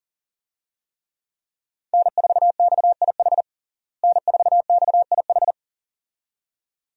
Morse code, heard as a keyed tone, about 40 wpm, 700 Hz. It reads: N4XIH N4XIH
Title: USA Call Signs - 5 Characters - Repeated 2x - 40wpm